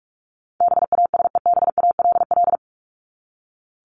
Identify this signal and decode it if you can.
Morse code, heard as a keyed tone, about 44 wpm, 700 Hz. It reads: BASEBALL
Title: Top 1751-2000 Words 45wpm